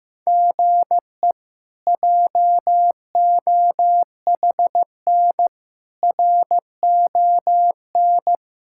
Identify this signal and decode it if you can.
Morse code, heard as a keyed tone, about 15 wpm, 700 Hz. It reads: GE JOHN RON
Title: K1USN SST QSO Practice - 15wpm